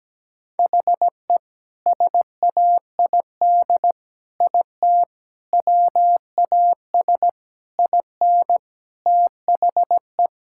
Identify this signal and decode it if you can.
Morse code, heard as a keyed tone, about 17 wpm, 700 Hz. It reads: HE SAID IT WAS IN THE
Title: Sentences from Top 1600 Words 17wpm